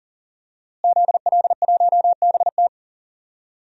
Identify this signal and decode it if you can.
Morse code, heard as a keyed tone, about 37 wpm, 700 Hz. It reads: ZL1BT
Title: DXCC-100 Call Signs - 5 Characters - Repeated 2x - 40wpm